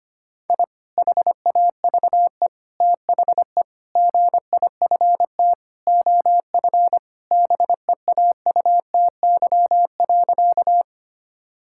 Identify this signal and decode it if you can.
Morse code, heard as a keyed tone, about 25 wpm, 700 Hz. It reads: I HAVE THE GIFT OF BEAUTY.
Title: Sentences from Top 1000 Words - Review Part 6 - 25wpm